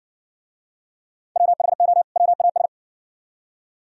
Morse code, heard as a keyed tone, about 60 wpm, 700 Hz. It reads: WHO WAS